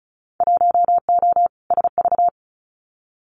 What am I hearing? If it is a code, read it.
Morse code, heard as a keyed tone, about 36 wpm, 700 Hz. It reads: 1O SV
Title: Winter Field Day Exchanges 35wpm